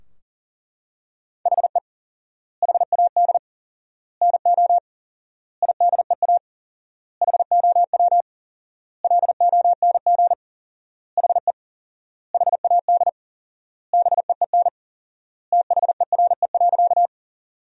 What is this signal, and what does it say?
Morse code, heard as a keyed tone, about 40 wpm, 700 Hz. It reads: HE HAD NO IDEA HOW LONG HE HAD BEEN THERE.
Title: 4x Word-Spacing - Sentences from Top 300 Words - 40wpm